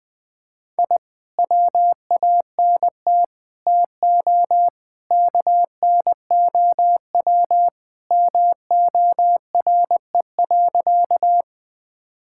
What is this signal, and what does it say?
Morse code, heard as a keyed tone, about 20 wpm, 700 Hz. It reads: I WANT TO KNOW MORE.
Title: Sentences with 2-Word Phrases - 20wpm